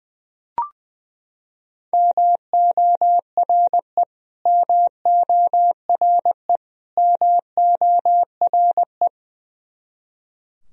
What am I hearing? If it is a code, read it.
Morse code, heard as a keyed tone, about 20 wpm, 700 Hz. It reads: MORE MORE MORE
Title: Top 100 Words Repeated 20wpm